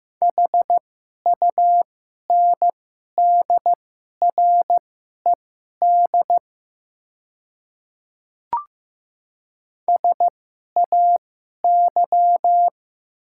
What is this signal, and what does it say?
Morse code, heard as a keyed tone, about 15 wpm, spaced as for 11 wpm, 700 Hz. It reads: HUNDRED SAY
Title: 2x Character Spacing - Top 500 Words - 15wpm